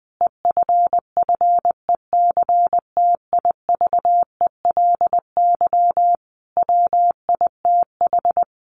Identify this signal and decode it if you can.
Morse code, heard as a keyed tone, about 20 wpm, 700 Hz. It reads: EFFECTIVELY WITH
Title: Sentences from Top 1400 Words 20wpm